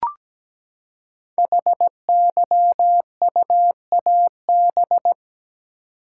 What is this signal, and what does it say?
Morse code, heard as a keyed tone, about 17 wpm, 700 Hz. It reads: HYUAB
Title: Code Groups - 5 Letter:Numbers - 17wpm